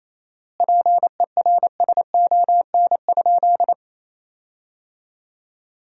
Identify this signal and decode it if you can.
Morse code, heard as a keyed tone, about 28 wpm, 700 Hz. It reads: PERSON?
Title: 4x Word-Spacing - Sentences from Top 400 Words - 28wpm